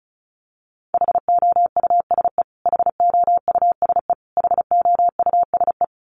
Morse code, heard as a keyed tone, about 34 wpm, 700 Hz. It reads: HOUSE HOUSE HOUSE
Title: Top 300 Words Repeated 35wpm